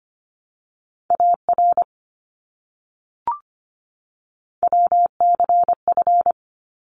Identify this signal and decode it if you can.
Morse code, heard as a keyed tone, about 25 wpm, 700 Hz. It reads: AR WCF
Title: ARRL Sections 25wpm